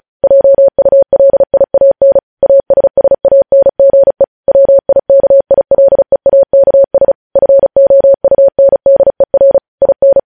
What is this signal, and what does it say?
Morse code, heard as a keyed tone, about 36 wpm, 550 Hz. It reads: JULIAN ASSANGE WIKILEAKS FOUNDER IN